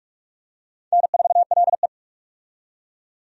Morse code, heard as a keyed tone, about 46 wpm, 700 Hz. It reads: N4LE
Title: USA Call Signs - 4 Characters - 45wpm